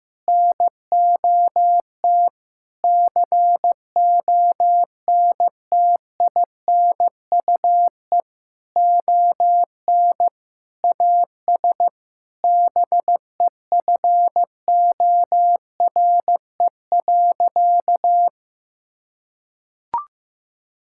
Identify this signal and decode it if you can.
Morse code, heard as a keyed tone, about 15 wpm, 700 Hz. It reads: NOT CONTINUE ON AS BEFORE.
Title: Sentences from Top 1000 Words - Review Part 6 - 15wpm